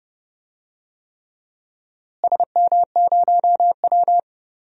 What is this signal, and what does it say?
Morse code, heard as a keyed tone, about 30 wpm, 700 Hz. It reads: SM0W
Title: DXCC-100 Call Signs - 4 Characters - Repeated 3x - 30wpm